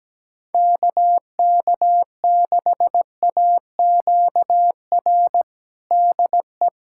Morse code, heard as a keyed tone, about 17 wpm, 700 Hz. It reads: KK6AQR DE
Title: USA Callsign DE Callsign 17wpm